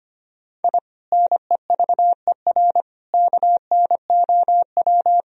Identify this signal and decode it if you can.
Morse code, heard as a keyed tone, about 25 wpm, 700 Hz. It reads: I NEVER KNOW
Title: Sentences from Top 1000 Words - Review Part 3 - 25wpm